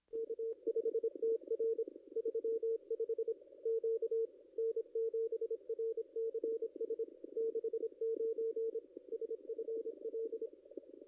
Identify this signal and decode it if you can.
Morse code, heard as a keyed tone, about 26 wpm, 450 Hz. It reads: K5TR 35 Q N7RCS 69 SFL
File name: contests / ss-cw-k5tr-op-n5rz -2200